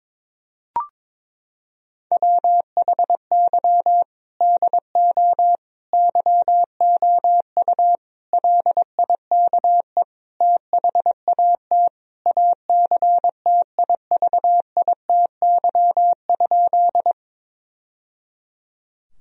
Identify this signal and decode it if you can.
Morse code, heard as a keyed tone, about 22 wpm, 700 Hz. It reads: WHY DO YOU LIKE THAT ACTIVITY?
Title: Sentences from Top 1000 Words - Review Part 4 - 22wpm